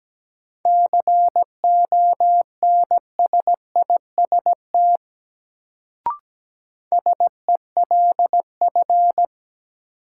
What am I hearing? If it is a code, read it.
Morse code, heard as a keyed tone, about 17 wpm, 700 Hz. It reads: CONSIST SELF
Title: Sentences from Top 1400 Words - New Word Drill Alt - 17wpm